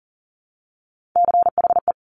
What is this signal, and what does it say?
Morse code, heard as a keyed tone, about 39 wpm, 700 Hz. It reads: CHE
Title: Top Three Letter Combinations in Words 40wpm